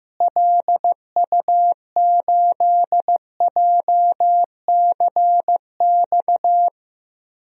Morse code, heard as a keyed tone, about 15 wpm, 700 Hz. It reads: LU8JCX
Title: DXCC-100 Call Signs - 6 Characters - Repeated 3x - 15wpm